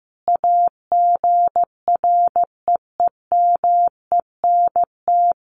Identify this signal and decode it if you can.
Morse code, heard as a keyed tone, about 14 wpm, 700 Hz. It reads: AGREEMENT
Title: Sentences from Top 1500 Words 15wpm